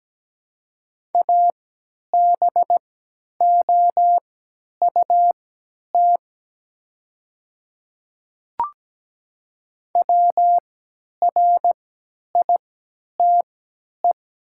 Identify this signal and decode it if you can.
Morse code, heard as a keyed tone, about 17 wpm, spaced as for 10 wpm, 690 Hz. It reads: ABOUT WRITE
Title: 3x Character Spacing - Top 100 Words - 17wpm